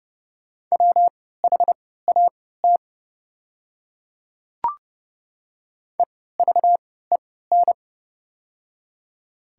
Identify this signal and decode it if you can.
Morse code, heard as a keyed tone, about 29 wpm, spaced as for 17 wpm, 700 Hz. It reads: WHAT EVEN
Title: 3x Character Spacing - Top 200 Words - 30wpm